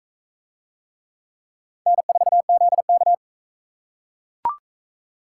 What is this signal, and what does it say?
Morse code, heard as a keyed tone, about 41 wpm, 700 Hz. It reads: N4ZK